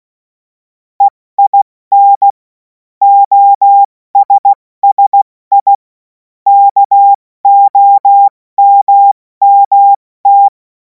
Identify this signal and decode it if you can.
Morse code, heard as a keyed tone, about 16 wpm, 800 Hz. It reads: EIN OSSI KOMMT
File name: Morsekurs / Witze / Witz 28. April 2026 70 F80